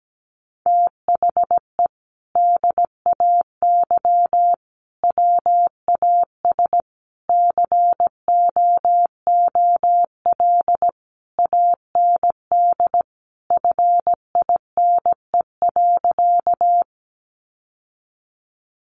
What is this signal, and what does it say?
Morse code, heard as a keyed tone, about 17 wpm, 700 Hz. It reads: THE DAY WAS COOL AND FINE.
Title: Sentences from Top 1200 Words 17wpm